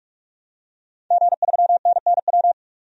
Morse code, heard as a keyed tone, about 46 wpm, 700 Hz. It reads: G3NNW
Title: DXCC-100 Call Signs - 5 Characters - 45wpm